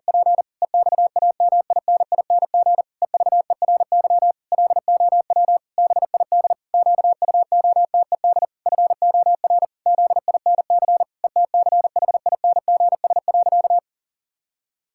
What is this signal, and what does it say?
Morse code, heard as a keyed tone, about 44 wpm, 700 Hz. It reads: P EXAMINING EVERY LOW BID QUOTED FOR ZINC ETCHINGS.
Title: Pangram Challenge v2 - 40wpm